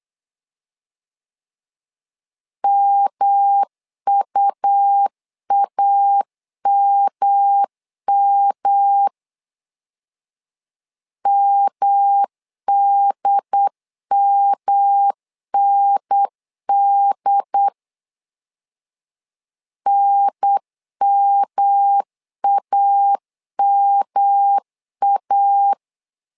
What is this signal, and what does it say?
Morse code, heard as a keyed tone, about 8 wpm, 800 Hz. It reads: MUAMM MDMND NMAMA
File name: LEZIONI / 3 MWRO / 3 M 8wpm 2300